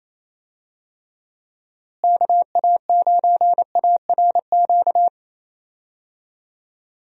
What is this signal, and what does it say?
Morse code, heard as a keyed tone, about 28 wpm, 700 Hz. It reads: KA9ARQ
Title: USA Call Signs - 6 Characters - Repeated 2x - 28wpm